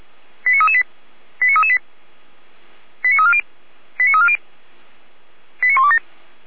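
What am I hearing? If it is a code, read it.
Five-tone selective call (ZVEI), heard as quick runs of five tones, tones of about 80 ms. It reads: 89338 89338 89340 89340 89127
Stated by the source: Received on an SDR; ZVEI five-tone selective call.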